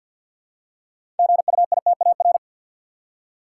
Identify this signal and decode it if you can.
Morse code, heard as a keyed tone, about 48 wpm, 700 Hz. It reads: GUITAR
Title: Six Letter Words - 50wpm